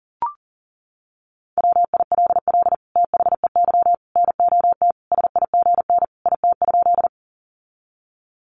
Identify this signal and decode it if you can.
Morse code, heard as a keyed tone, about 40 wpm, 700 Hz. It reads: WILL THEY NOT SIGN IT?
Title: Sentences from Top 1000 Words 40wpm